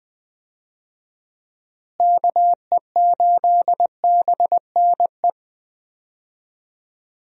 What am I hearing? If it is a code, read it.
Morse code, heard as a keyed tone, about 20 wpm, 700 Hz. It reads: KE8BNE
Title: USA Call Signs - 6 Characters - Repeated 2x - 20wpm